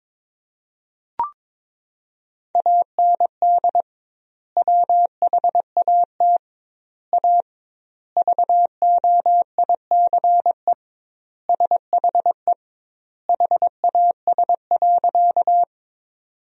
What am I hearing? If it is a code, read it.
Morse code, heard as a keyed tone, about 22 wpm, 700 Hz. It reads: AND WHAT A VOICE SHE HAS.
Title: 2x Word-Spacing - Sentences from Top 500 Words - 22wpm